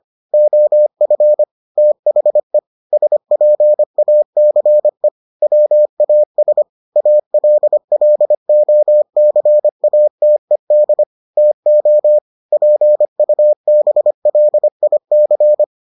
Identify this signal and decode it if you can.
Morse code, heard as a keyed tone, about 25 wpm, 600 Hz. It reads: OF THE SPACE WAS ALLOCATED TO PUBLIC